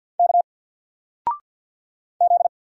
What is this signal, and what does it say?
Morse code, heard as a keyed tone, about 50 wpm, 700 Hz. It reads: K Z
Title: Single Letter-Number 50wpm